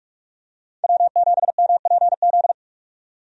Morse code, heard as a keyed tone, about 43 wpm, 700 Hz. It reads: W7MPZ
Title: USA Call Signs - 5 Characters - 45wpm